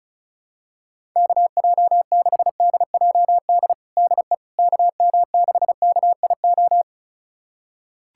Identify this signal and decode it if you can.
Morse code, heard as a keyed tone, about 36 wpm, 700 Hz. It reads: KJ6DJD DE KM6KIO
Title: USA Callsign DE Callsign 35wpm